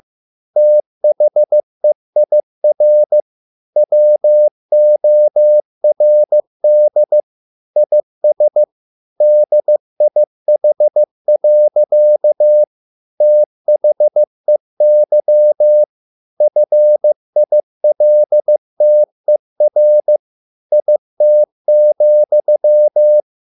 Morse code, heard as a keyed tone, about 15 wpm, 600 Hz. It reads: THEIR WORD IS DIH. THEY FILTER IT,